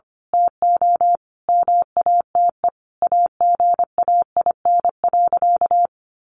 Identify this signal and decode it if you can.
Morse code, heard as a keyed tone, about 25 wpm, 700 Hz. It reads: TO MATE AGAIN.
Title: A Sound Of Thunder 25wpm